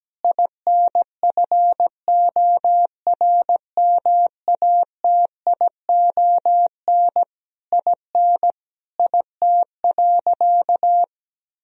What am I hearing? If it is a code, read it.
Morse code, heard as a keyed tone, about 17 wpm, 700 Hz. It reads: INFORMATION IN IT.
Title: Sentences from Top 1000 Words - Review Part 4 - 17wpm